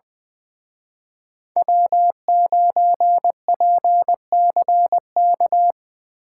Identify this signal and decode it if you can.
Morse code, heard as a keyed tone, about 24 wpm, 700 Hz. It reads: W9PCK
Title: Speed Racing - USA 1x3 Call Signs 20wpm